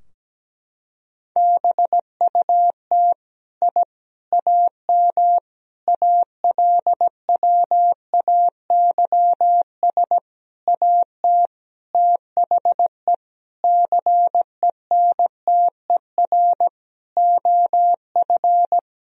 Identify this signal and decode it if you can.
Morse code, heard as a keyed tone, about 17 wpm, 700 Hz. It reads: BUT I AM ALWAYS AT THE CENTER OF
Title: Sentences from Top 1000 Words - Review Part 2 - 17wpm